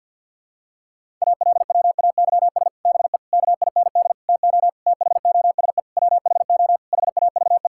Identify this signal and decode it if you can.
Morse code, heard as a keyed tone, about 50 wpm, 700 Hz. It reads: ALWAYS BE KIND TO THOSE WHO HAVE
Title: Sentences from Top 1600 Words 50wpm